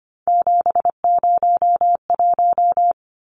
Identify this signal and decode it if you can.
Morse code, heard as a keyed tone, about 25 wpm, 700 Hz. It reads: 701
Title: Numbers - 3-Digit 25wpm